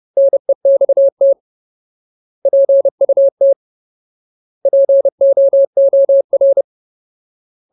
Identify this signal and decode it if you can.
Morse code, heard as a keyed tone, about 30 wpm, 550 Hz. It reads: NEXT PUT POOR
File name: Ultimate Word List / 30WPM Ultimate